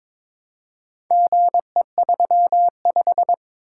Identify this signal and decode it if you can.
Morse code, heard as a keyed tone, about 22 wpm, 700 Hz. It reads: GE35
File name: practice / 22wpm / Four Letter-Number Combinations - 22wpm